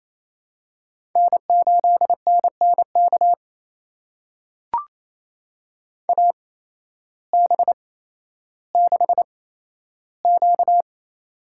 Morse code, heard as a keyed tone, about 28 wpm, spaced as for 13 wpm, 700 Hz. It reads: N8NNK AB6Q
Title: USA Call Signs - 8x Character Spacing - 28wpm